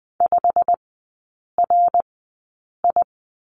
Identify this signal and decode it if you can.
Morse code, heard as a keyed tone, about 20 wpm, spaced as for 8 wpm, 700 Hz. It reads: 5RI
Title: 087 Lesson 15e - Number 5 - Three Characters 20wpm